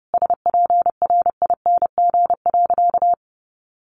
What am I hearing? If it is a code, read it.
Morse code, heard as a keyed tone, about 30 wpm, 700 Hz. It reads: SPRING.
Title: Sentences from Top 1200 Words 30wpm